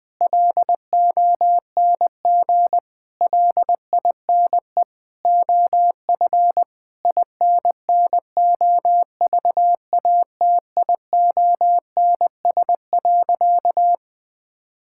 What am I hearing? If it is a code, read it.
Morse code, heard as a keyed tone, about 20 wpm, 700 Hz. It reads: LONG LINE OF INNOVATIONS.
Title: Sentences with 8-Word Phrases - Spoken Beginning - 20wpm